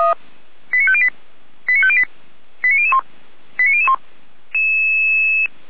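Five-tone selective call (ZVEI), heard as quick runs of five tones, tones of about 80 ms. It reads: 89508 89508 89001 89001
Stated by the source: ZVEI five-tone selective call 89160 received by SDR, recorded by Feuerwehr Orlamünde.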